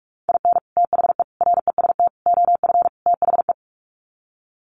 Morse code, heard as a keyed tone, about 44 wpm, 700 Hz. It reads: IN THE REST OF THE
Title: Sets of 5 Words 45wpm